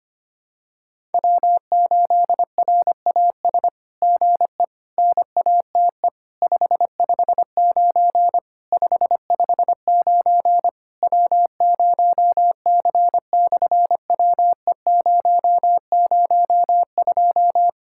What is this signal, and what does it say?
Morse code, heard as a keyed tone, about 25 wpm, 700 Hz. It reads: W8RAS GE NATE 559 559 W0C/WE002
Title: SOTA QSO Practice 25wpm